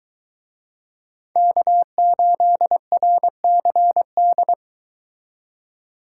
Morse code, heard as a keyed tone, about 23 wpm, 700 Hz. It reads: K8RCD